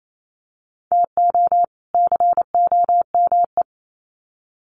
Morse code, heard as a keyed tone, about 28 wpm, 700 Hz. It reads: TO COME
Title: Sets of 2 Words 28wpm